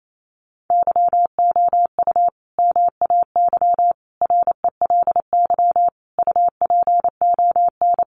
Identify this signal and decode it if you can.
Morse code, heard as a keyed tone, about 28 wpm, 700 Hz. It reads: YOU MAY RELY UPON
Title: Sentences from Top 1200 Words 28wpm